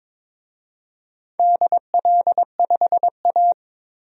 Morse code, heard as a keyed tone, about 22 wpm, 700 Hz. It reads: DL5A
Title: DXCC-100 Call Signs - 4 Characters - 22wpm